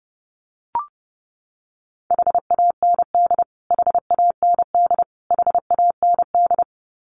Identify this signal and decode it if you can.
Morse code, heard as a keyed tone, about 30 wpm, 700 Hz. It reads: HAND HAND HAND
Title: Top 500 Words Repeated 30wpm